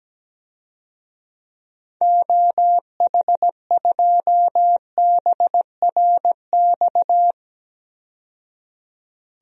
Morse code, heard as a keyed tone, about 17 wpm, 700 Hz. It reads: OH2BRX
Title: DXCC-100 Call Signs - 6 Characters - 17wpm